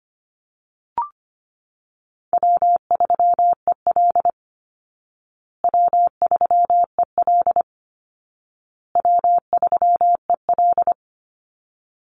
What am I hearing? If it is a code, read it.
Morse code, heard as a keyed tone, about 25 wpm, 700 Hz. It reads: W3EL W3EL W3EL
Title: Easy Call Signs Repeated 3x - 25wpm